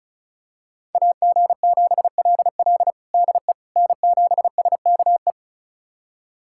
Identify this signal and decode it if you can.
Morse code, heard as a keyed tone, about 34 wpm, 700 Hz. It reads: AG7LL DE N7SKE
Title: USA Callsign DE Callsign - 5 Characters - 35wpm